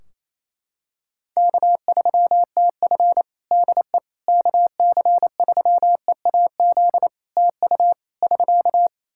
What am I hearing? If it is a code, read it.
Morse code, heard as a keyed tone, about 27 wpm, 700 Hz. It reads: K3TF DE KC3EAZ TU <SK>
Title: Practice QSO - Exchange 4 - Ending - 28wpm